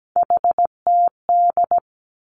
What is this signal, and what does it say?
Morse code, heard as a keyed tone, about 17 wpm, 700 Hz. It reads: HTD
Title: Three Challenging Letter-Number Combinations 17wpm